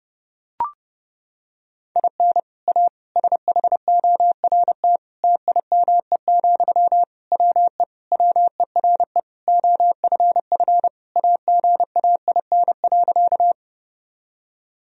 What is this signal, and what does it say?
Morse code, heard as a keyed tone, about 30 wpm, 700 Hz. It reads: IN A SHORT TIME, WE WERE OFF AGAIN.
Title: Sentences from Top 1000 Words - Review Part 2 - 30wpm